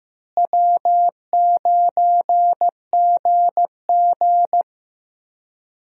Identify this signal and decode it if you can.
Morse code, heard as a keyed tone, about 15 wpm, 700 Hz. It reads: W9GG
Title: USA Call Signs - 4 Characters - Repeated 3x - 15wpm